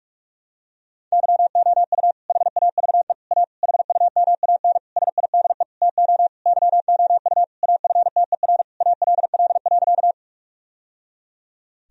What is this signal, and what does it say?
Morse code, heard as a keyed tone, about 45 wpm, 700 Hz. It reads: YOU HAVE A HUMAN SIDE TO YOU AFTER ALL.
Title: Sentences from Top 1000 Words - Review Part 3 - 45wpm